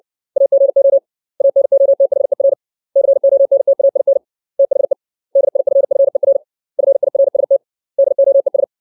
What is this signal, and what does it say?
Morse code, heard as a keyed tone, about 61 wpm, 550 Hz. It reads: AGO ANOTHER CONTAIN THE DIFFER FIRST DOES
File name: 632 words / 632 Words 60WPM